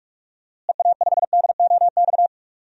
Morse code, heard as a keyed tone, about 46 wpm, 700 Hz. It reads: EA5DOX
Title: DXCC-100 Call Signs - 6 Characters - Repeated 2x - 45wpm